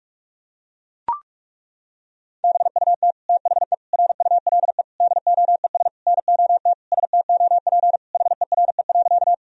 Morse code, heard as a keyed tone, about 41 wpm, 700 Hz. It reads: BUT THE RULE DOES NOT STOP HERE.
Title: Sentences from Top 1000 Words - Review Part 2 - 45wpm